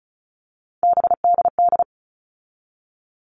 Morse code, heard as a keyed tone, about 34 wpm, 700 Hz. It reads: BDD
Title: All Cut Numbers - 3-Digit 35wpm